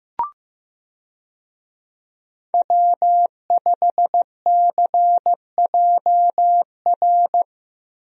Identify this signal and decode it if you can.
Morse code, heard as a keyed tone, about 15 wpm, 700 Hz. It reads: W5CJR
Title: USA Call Signs - 5 Characters - Repeated 3x - 15wpm